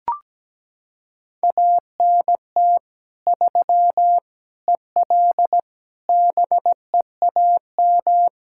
Morse code, heard as a keyed tone, about 17 wpm, 700 Hz. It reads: ANT 3 EL BEAM
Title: K1USN SST QSO Practice 17wpm